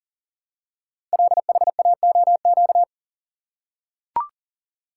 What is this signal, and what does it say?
Morse code, heard as a keyed tone, about 39 wpm, 690 Hz. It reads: LHAOQ
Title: Code Groups - 5 Letters - 40wpm